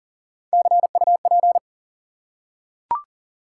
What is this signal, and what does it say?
Morse code, heard as a keyed tone, about 39 wpm, 700 Hz. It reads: CUP
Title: Three Letter Words - 40wpm